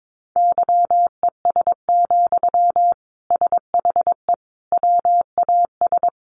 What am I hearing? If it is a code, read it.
Morse code, heard as a keyed tone, about 22 wpm, 700 Hz. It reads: YES, SHE WAS
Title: Sentences from Top 500 Words 22wpm